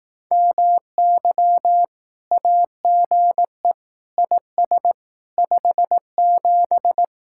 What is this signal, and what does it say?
Morse code, heard as a keyed tone, about 18 wpm, 700 Hz. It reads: MY AGE IS 57
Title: Speed-Racing Common QSO Phrases Encore - 15wpm